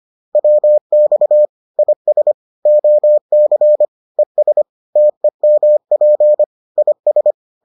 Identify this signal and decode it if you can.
Morse code, heard as a keyed tone, about 25 wpm, 600 Hz. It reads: WX IS OC ES TEMP IS